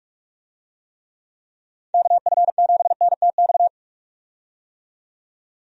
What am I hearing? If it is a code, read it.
Morse code, heard as a keyed tone, about 42 wpm, 700 Hz. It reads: KF7NTX
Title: USA Call Signs - 6 Characters - 45wpm